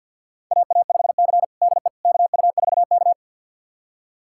Morse code, heard as a keyed tone, about 52 wpm, 700 Hz. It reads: AA5C DE KU4K
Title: USA Callsign DE Callsign - 4 Characters - 50wpm